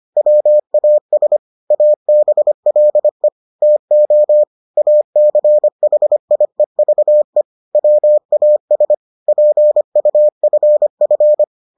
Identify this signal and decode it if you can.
Morse code, heard as a keyed tone, about 25 wpm, 600 Hz. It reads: WAS ABLE TO ACHIEVE WAS PUFF